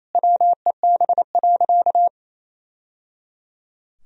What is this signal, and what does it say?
Morse code, heard as a keyed tone, about 28 wpm, 700 Hz. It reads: WEB.
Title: Sentences from Top 1000 Words - Review Part 4 - 28wpm